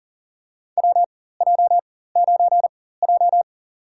Morse code, heard as a keyed tone, about 40 wpm, spaced as for 19 wpm, 700 Hz. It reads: WJ9J
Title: USA Call Signs - 4 Characters - 4x Character Spacing - 40wpm